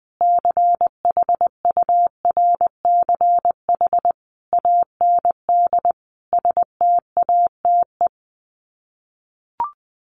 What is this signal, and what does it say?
Morse code, heard as a keyed tone, about 20 wpm, 700 Hz. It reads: CHURCH AND STATE
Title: Sets of 5 Words 20wpm